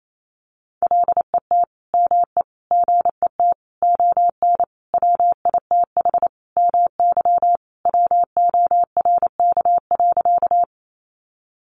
Morse code, heard as a keyed tone, about 28 wpm, 700 Hz. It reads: LET ME GET ON WITH MY WORK.
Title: Sentences from Top 300 Words 28wpm